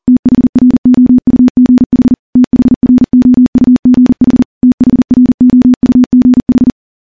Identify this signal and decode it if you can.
Morse code, heard as a keyed tone, about 40 wpm, 250 Hz. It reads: THROUGH THROUGH THROUGH
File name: morse-training-40WPM